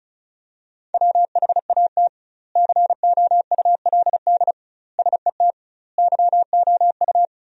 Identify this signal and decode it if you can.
Morse code, heard as a keyed tone, about 34 wpm, 700 Hz. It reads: WHAT COULD SET YOU
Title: 2x Word-Spacing - Sentences from Top 200 Words - 35wpm